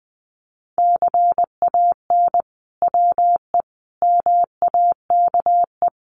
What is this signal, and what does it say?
Morse code, heard as a keyed tone, about 20 wpm, 700 Hz. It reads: CAN WE MAKE
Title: Sentences from Top 1500 Words 20wpm